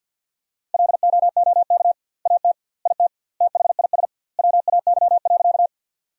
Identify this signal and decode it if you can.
Morse code, heard as a keyed tone, about 50 wpm, 700 Hz. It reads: LOOK AT IT THIS WAY.
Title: 2x Word-Spacing - Sentences from Top 100 Words - 50wpm